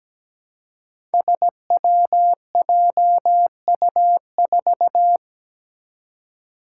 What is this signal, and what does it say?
Morse code, heard as a keyed tone, about 17 wpm, 700 Hz. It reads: SWJU4
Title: Five Challenging Letter-Number Combinations 17wpm